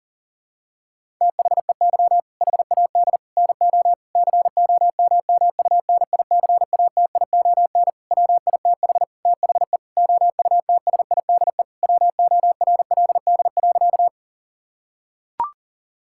Morse code, heard as a keyed tone, about 40 wpm, 700 Hz. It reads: THEY HAD NO COMMUNICATION WITH THE OUTSIDE WORLD.
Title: Sentences from Top 1000 Words - Review Part 6 - 40wpm